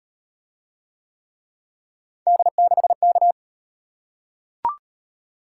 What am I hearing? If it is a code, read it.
Morse code, heard as a keyed tone, about 38 wpm, 700 Hz. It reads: D6K